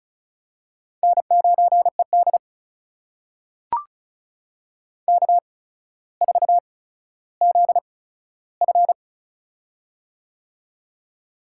Morse code, heard as a keyed tone, about 36 wpm, spaced as for 15 wpm, 700 Hz. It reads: N9ED K4ZF
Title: USA Call Signs - 4 Characters - 8x Character Spacing - 35wpm